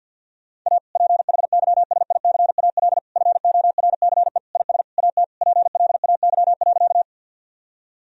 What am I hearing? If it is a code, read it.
Morse code, heard as a keyed tone, about 50 wpm, 700 Hz. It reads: A PHYSICAL FORCE IS AT PLAY.
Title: Sentences from Top 1000 Words - Review Part 4 - 50wpm